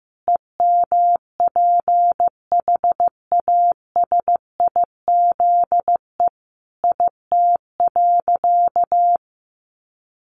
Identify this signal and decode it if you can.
Morse code, heard as a keyed tone, about 15 wpm, 700 Hz. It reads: EMPHASIZE IT.
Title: Sentences from Top 1500 Words 15wpm